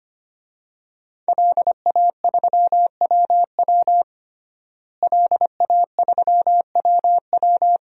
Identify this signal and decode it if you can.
Morse code, heard as a keyed tone, about 25 wpm, 700 Hz. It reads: LA3WW LA3WW
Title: DXCC-100 Call Signs - 5 Characters - Repeated 2x - 25wpm